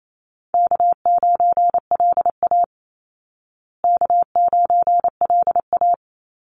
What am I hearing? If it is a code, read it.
Morse code, heard as a keyed tone, about 28 wpm, 700 Hz. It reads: K9LA K9LA
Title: Easy Call Signs Repeated 2x - 28wpm